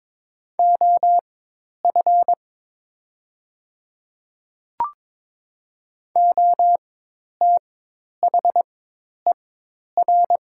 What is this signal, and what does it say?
Morse code, heard as a keyed tone, about 22 wpm, spaced as for 10 wpm, 700 Hz. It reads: OF OTHER
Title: 4x Character Spacing - Top 100 Words - 22wpm